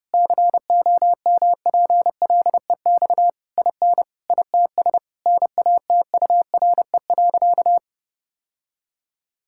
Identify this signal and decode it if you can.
Morse code, heard as a keyed tone, about 30 wpm, 700 Hz. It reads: COMPLEX IN ITS NATURE.
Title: Sentences from Top 1000 Words - Review Part 6 - 30wpm